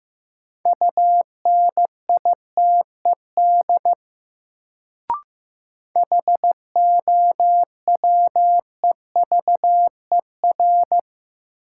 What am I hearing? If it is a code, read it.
Morse code, heard as a keyed tone, about 15 wpm, 700 Hz. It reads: UNITED HOWEVER
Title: Sentences from Top 600 Words - New Word Drill - 15wpm